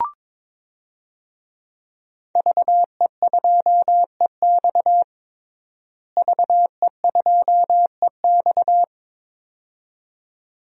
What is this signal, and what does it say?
Morse code, heard as a keyed tone, about 22 wpm, 700 Hz. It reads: VE2EX VE2EX
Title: DXCC-100 Call Signs - 5 Characters - Repeated 2x - 22wpm